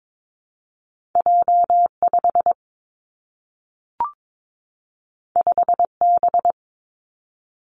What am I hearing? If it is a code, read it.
Morse code, heard as a keyed tone, about 22 wpm, 700 Hz. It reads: J 5 5 B
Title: Two Challenging Letter-Number Combinations 22wpm